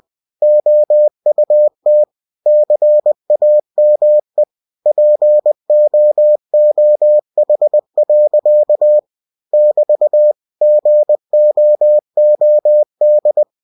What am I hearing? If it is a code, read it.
Morse code, heard as a keyed tone, about 20 wpm, 600 Hz. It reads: OUT CAME POOH. <BT> GOOD
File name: cw-books / winnie-the-pooh / winnie-20wpm-page-0044